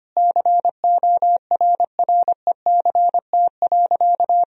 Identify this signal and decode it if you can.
Morse code, heard as a keyed tone, about 25 wpm, 700 Hz. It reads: CORRECT.
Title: Sentences from Top 1000 Words - Review Part 2 - 25wpm